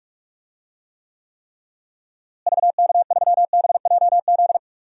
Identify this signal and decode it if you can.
Morse code, heard as a keyed tone, about 46 wpm, 700 Hz. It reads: VK3BJZ
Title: DXCC-100 Call Signs - 6 Characters - Repeated 3x - 45wpm